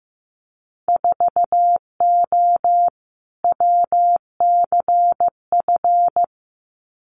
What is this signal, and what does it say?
Morse code, heard as a keyed tone, about 15 wpm, 700 Hz. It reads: 4O WCF
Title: Winter Field Day Exchanges Encore 15wpm